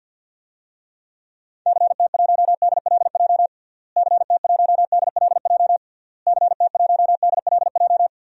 Morse code, heard as a keyed tone, about 52 wpm, 700 Hz. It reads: CT1DLJ CT1DLJ CT1DLJ
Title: DXCC-100 Call Signs - 6 Characters - Repeated 3x - 50wpm